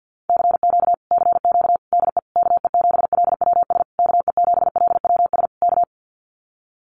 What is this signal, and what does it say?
Morse code, heard as a keyed tone, about 50 wpm, 700 Hz. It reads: CQ CQ DE KE7LWS KE7LWS K
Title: Common QSO Phrases Encore 50wpm